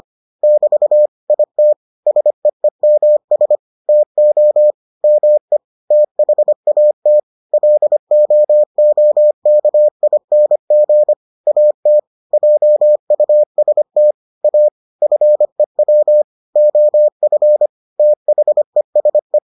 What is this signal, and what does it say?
Morse code, heard as a keyed tone, about 25 wpm, 600 Hz. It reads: <BT> IT SEEMS TO ME THAT LOOKING AT JUST A FEW OF THESE